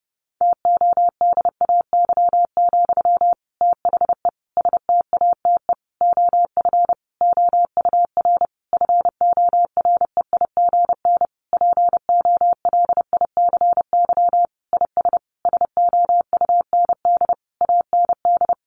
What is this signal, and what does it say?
Morse code, heard as a keyed tone, about 30 wpm, 700 Hz. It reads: TODAY, THE STATE OF OUR FOREIGN POLICY IS SOUND AND
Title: Sentences from Top 1400 Words 30wpm